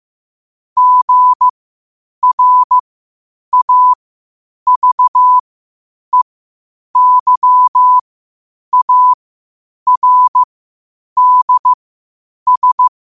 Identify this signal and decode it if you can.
Morse code, heard as a keyed tone, about 15 wpm, spaced as for 8 wpm, 1000 Hz. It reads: GRAVEYARDS
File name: qotdfiles / 8wpm